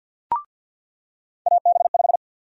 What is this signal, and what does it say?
Morse code, heard as a keyed tone, about 47 wpm, 700 Hz. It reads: AB5
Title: USA Call Sign Prefixes - 50wpm